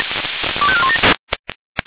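Five-tone selective call (ZVEI), heard as quick runs of five tones, tones of about 70 ms. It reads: 25417
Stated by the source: ZVEI five-tone call received on an SDR.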